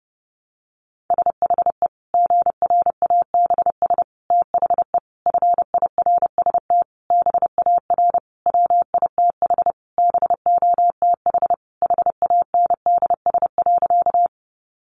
Morse code, heard as a keyed tone, about 30 wpm, 700 Hz. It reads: SHE GRABS THE FIRST BAR WITH BOTH HANDS.
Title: Easy Phrases 30wpm